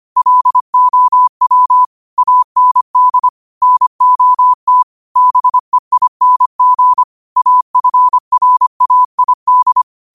Morse code, heard as a keyed tone, about 25 wpm, 1000 Hz. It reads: LOW AND NOT BEING AFRAID
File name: qotdfiles / 25wpm